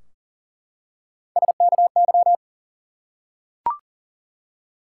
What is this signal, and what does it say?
Morse code, heard as a keyed tone, about 40 wpm, 700 Hz. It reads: SKY
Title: Three Letter Words - 40wpm